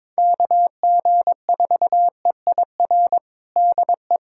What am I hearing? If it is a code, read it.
Morse code, heard as a keyed tone, about 22 wpm, 700 Hz. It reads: KG4EIR DE
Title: USA Callsign DE Callsign 22wpm